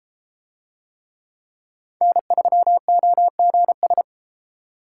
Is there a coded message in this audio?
Yes: Morse code from 2.0 to 4.0 s, heard as a keyed tone, about 30 wpm, 700 Hz. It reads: N3OGS